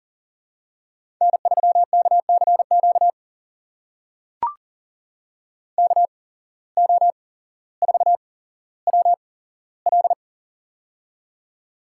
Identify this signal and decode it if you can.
Morse code, heard as a keyed tone, about 37 wpm, spaced as for 17 wpm, 700 Hz. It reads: N3KCQ KO4WL
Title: USA Call Signs - 5 Characters - 8x Character Spacing - 40wpm